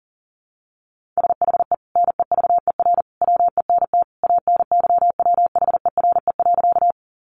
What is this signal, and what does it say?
Morse code, heard as a keyed tone, about 40 wpm, 700 Hz. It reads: SHE NEVER WENT ANYWHERE.
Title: Sentences from Top 1400 Words 40wpm